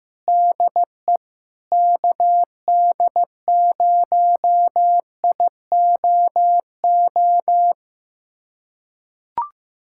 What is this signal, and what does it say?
Morse code, heard as a keyed tone, about 16 wpm, 700 Hz. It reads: DE KD0IOO
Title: USA Callsign DE Callsign - 6 Characters - 15wpm